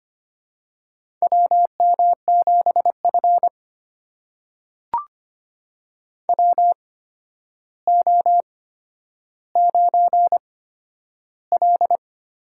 Morse code, heard as a keyed tone, about 25 wpm, spaced as for 11 wpm, 700 Hz. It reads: WM7F WO9L
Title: USA Call Signs - 4 Characters - 8x Character Spacing - 25wpm